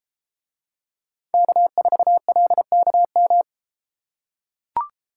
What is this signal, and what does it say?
Morse code, heard as a keyed tone, about 33 wpm, 700 Hz. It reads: K4LKM